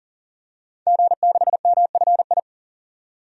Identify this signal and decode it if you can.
Morse code, heard as a keyed tone, about 40 wpm, 700 Hz. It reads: G6MFI
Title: DXCC-100 Call Signs - 5 Characters - 40wpm